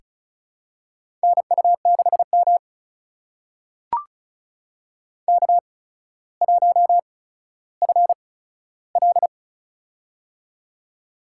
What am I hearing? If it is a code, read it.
Morse code, heard as a keyed tone, about 36 wpm, spaced as for 15 wpm, 700 Hz. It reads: NU6M K1FL
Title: USA Call Signs - 4 Characters - 8x Character Spacing - 35wpm